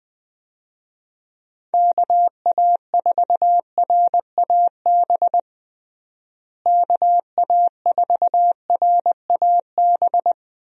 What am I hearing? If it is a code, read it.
Morse code, heard as a keyed tone, about 20 wpm, 700 Hz. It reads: KA4RAB KA4RAB
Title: USA Call Signs - 6 Characters - Repeated 2x - 20wpm